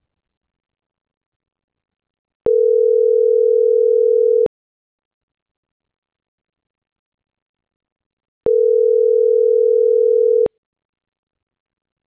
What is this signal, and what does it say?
Ringback tone (North America), 440 + 480 Hz.